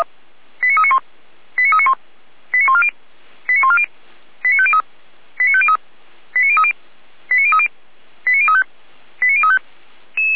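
Five-tone selective call (ZVEI), heard as quick runs of five tones, tones of about 70 ms. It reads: 89391 89391 89140 89140 89503 89503 89030 89030 89035 89035
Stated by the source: Five-tone ZVEI selective call 89050, received on SDR.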